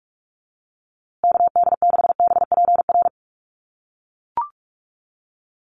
Morse code, heard as a keyed tone, about 44 wpm, 700 Hz. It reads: KD6BPR
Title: USA Call Signs Repeated 2x 45wpm